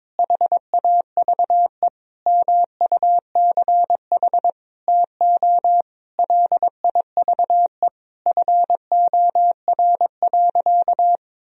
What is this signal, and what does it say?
Morse code, heard as a keyed tone, about 22 wpm, 700 Hz. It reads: HAVE MUCH TO LIVE FOR.
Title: Sentences from Top 1000 Words - Review Part 3 - 22wpm